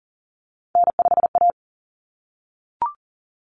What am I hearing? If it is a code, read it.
Morse code, heard as a keyed tone, about 40 wpm, 700 Hz. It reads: N5A
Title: Common Cut Numbers - 3-Digit 40wpm